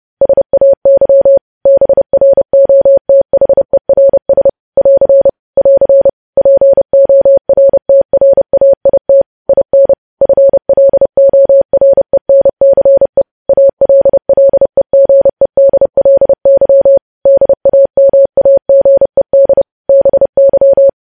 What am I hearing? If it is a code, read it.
Morse code, heard as a keyed tone, about 30 wpm, 550 Hz. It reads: SAY BROTHERS <AR> <AR> PORTRAIT IN FLORENCE ALLEGEDLY DAMAGED BY
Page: news headlines in morse code at 30 wpm